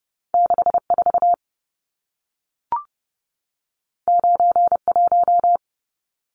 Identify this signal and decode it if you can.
Morse code, heard as a keyed tone, about 30 wpm, 700 Hz. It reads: 64 91
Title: Numbers - 2-Digit 30wpm